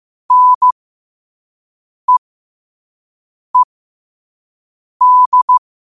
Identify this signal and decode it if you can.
Morse code, heard as a keyed tone, about 15 wpm, spaced as for 5 wpm, 1000 Hz. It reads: NEED
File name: qotdfiles / 5wpm